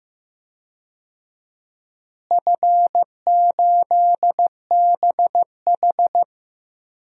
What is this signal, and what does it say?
Morse code, heard as a keyed tone, about 15 wpm, 700 Hz. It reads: F8BH
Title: DXCC-100 Call Signs - 4 Characters - Repeated 3x - 15wpm